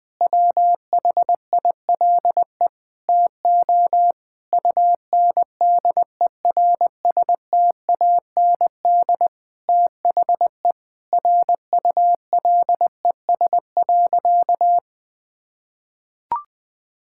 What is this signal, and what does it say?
Morse code, heard as a keyed tone, about 20 wpm, 700 Hz. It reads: WHILE TO UNDERSTAND THE RULES.
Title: Sentences with 6-Word Phrases - 20wpm